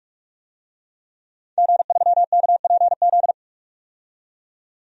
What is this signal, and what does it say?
Morse code, heard as a keyed tone, about 46 wpm, 700 Hz. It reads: G3KPZ
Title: DXCC-100 Call Signs - 5 Characters - 45wpm